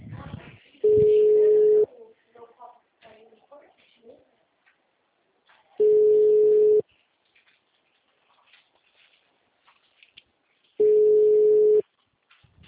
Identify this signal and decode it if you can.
Ringback tone (Europe, 425 Hz), with speech.
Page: Limeira 4 - SP